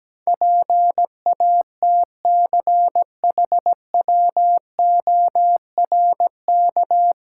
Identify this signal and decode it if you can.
Morse code, heard as a keyed tone, about 17 wpm, 700 Hz. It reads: PATCHWORK
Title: Pangram Challenge v2 - 17wpm